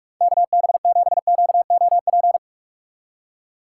Morse code, heard as a keyed tone, about 46 wpm, 700 Hz. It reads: KB7QOP
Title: K1USN SST QSO Practice - 45wpm